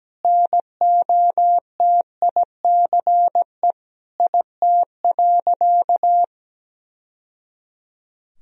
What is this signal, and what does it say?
Morse code, heard as a keyed tone, about 17 wpm, 700 Hz. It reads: NOTICE IT.
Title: Sentences from Top 1000 Words - Review Part 2 - 17wpm